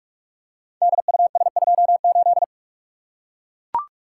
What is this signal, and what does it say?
Morse code, heard as a keyed tone, about 42 wpm, 700 Hz. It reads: DUS28
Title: Code Groups - 5 Letter:Numbers - 45wpm